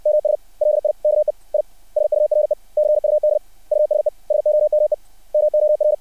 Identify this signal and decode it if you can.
Morse code, heard as a keyed tone, about 60 wpm, 600 Hz. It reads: MS LIKE I SURE CUD USE SOME MOR
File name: archived / CRSnet